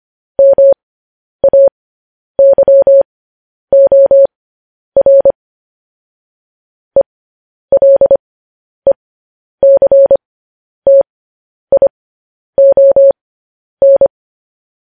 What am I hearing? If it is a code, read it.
Morse code, heard as a keyed tone, about 25 wpm, spaced as for 10 wpm, 550 Hz. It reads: MAYOR ELECTION
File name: podcast / News headlines-1x-25x10